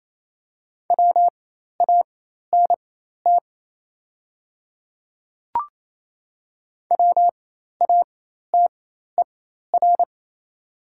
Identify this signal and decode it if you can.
Morse code, heard as a keyed tone, about 28 wpm, spaced as for 13 wpm, 700 Hz. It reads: WANT WATER
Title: 4x Character Spacing - Top 200 Words - 28wpm